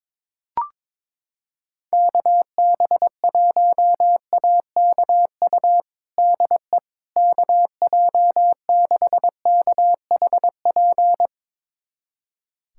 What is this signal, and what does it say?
Morse code, heard as a keyed tone, about 22 wpm, 700 Hz. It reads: KB1AKU DE KJ6KHP
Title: USA Callsign DE Callsign - 6 Characters - 22wpm